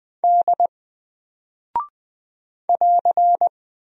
Morse code, heard as a keyed tone, about 20 wpm, 700 Hz. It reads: D <AR>
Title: Single Character (Letter:Number:Punctuation:Prosign) 20wpm